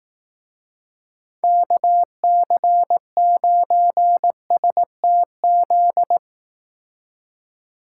Morse code, heard as a keyed tone, about 18 wpm, 700 Hz. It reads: KC9STZ
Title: Speed Racing - USA 2x3 Call Signs 15wpm